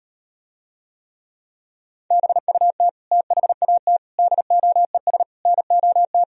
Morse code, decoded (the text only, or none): BUT THAT DOES NOT